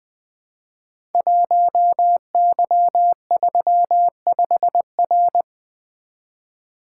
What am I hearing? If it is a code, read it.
Morse code, heard as a keyed tone, about 20 wpm, 700 Hz. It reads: 1Y35R
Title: Code Groups - 5 Letter:Numbers - 20wpm